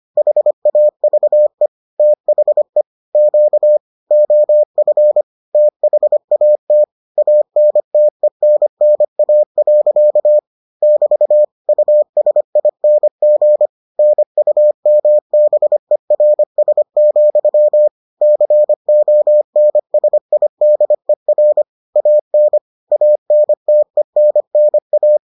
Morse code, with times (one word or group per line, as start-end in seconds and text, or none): 0.2-1.7 HAVE
2.0-2.8 THE
3.1-3.8 Q
4.1-5.2 OF
5.5-6.8 THAT
7.2-10.4 ANTENNA.
10.8-11.4 <BT>
11.7-13.7 USING
14.0-17.9 NUMBERS,
18.2-21.6 CONSIDER
22.0-22.6 AN
22.9-25.2 ANTENNA